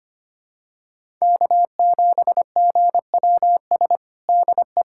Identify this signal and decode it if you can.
Morse code, heard as a keyed tone, about 25 wpm, 700 Hz. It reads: K7GWS DE
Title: USA Callsign DE Callsign - 5 Characters - 25wpm